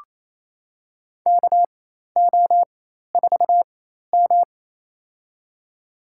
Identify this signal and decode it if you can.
Morse code, heard as a keyed tone, about 28 wpm, spaced as for 13 wpm, 700 Hz. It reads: KO4M
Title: USA Call Signs - 4x Character Spacing - 28wpm